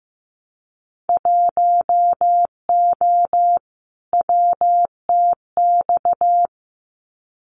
Morse code, heard as a keyed tone, about 15 wpm, 700 Hz. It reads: 1O WTX
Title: Winter Field Day Exchanges Encore 15wpm